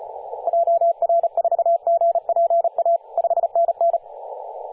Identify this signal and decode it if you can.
Morse code, heard as a keyed tone, about 35 wpm, 680 Hz, with radio noise. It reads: JR4GPA 5NN
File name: posts / 2016-sep-22-0537z-kh0-kw2x-18-077mhz-cw